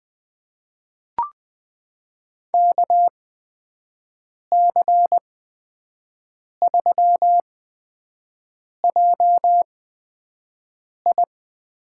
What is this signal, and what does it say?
Morse code, heard as a keyed tone, about 20 wpm, spaced as for 5 wpm, 700 Hz. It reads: KC3JI
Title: USA Call Signs - 6 Characters - 8x Character Spacing - 20wpm